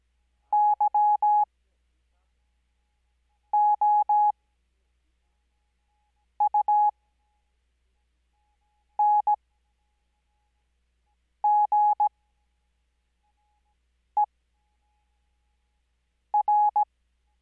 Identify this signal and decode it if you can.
Morse code, heard as a keyed tone, about 17 wpm, spaced as for 4 wpm, 840 Hz. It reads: YOUNGER
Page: Farnsworth Spacing: 15WPM Character speed Spaced at 5WPM.